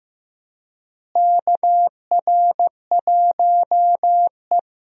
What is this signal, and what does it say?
Morse code, heard as a keyed tone, about 15 wpm, 700 Hz. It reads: KR1E
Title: USA Call Signs - 4 Characters - Repeated 2x - 15wpm